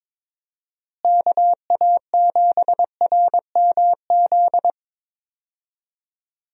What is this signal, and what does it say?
Morse code, heard as a keyed tone, about 22 wpm, 700 Hz. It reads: KA7RMZ
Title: USA Call Signs - 6 Characters - Repeated 2x - 22wpm